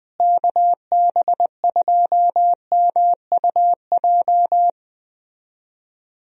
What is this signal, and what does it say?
Morse code, heard as a keyed tone, about 20 wpm, 700 Hz. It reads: KB2MUJ
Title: USA Call Signs - 6 Characters - Repeated 3x - 20wpm